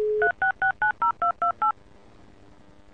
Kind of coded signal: DTMF